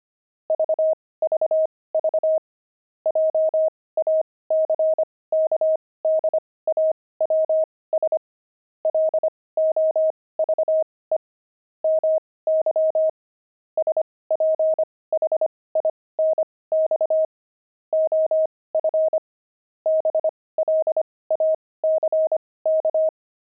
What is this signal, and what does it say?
Morse code, heard as a keyed tone, about 25 wpm, spaced as for 18 wpm, 630 Hz. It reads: VVV JACKDAWS LOVE MY SPHINX OF BLACK
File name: pangram-fs-18-cs-25